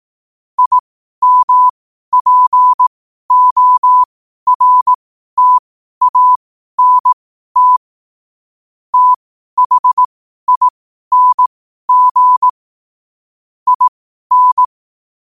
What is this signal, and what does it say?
Morse code, heard as a keyed tone, about 18 wpm, spaced as for 12 wpm, 1000 Hz. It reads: IMPORTANT THING IN
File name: qotdfiles / 12wpm